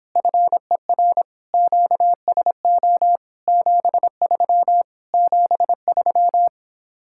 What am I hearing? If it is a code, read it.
Morse code, heard as a keyed tone, about 26 wpm, 700 Hz. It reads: FER QSO 73 73